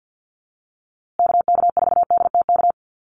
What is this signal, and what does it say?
Morse code, heard as a keyed tone, about 50 wpm, 700 Hz. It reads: KK4DTK
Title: USA Call Signs Repeated 2x 50wpm